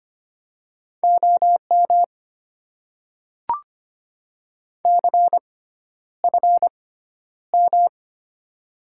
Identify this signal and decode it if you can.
Morse code, heard as a keyed tone, about 25 wpm, spaced as for 11 wpm, 700 Hz. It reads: OM CFM
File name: Top 90 QSO Elements - 6x Character Spacing - 25wpm